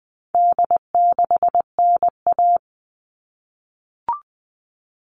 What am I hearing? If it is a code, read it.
Morse code, heard as a keyed tone, about 20 wpm, 700 Hz. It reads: D6NA
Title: All Cut Numbers - 4-Digit 20wpm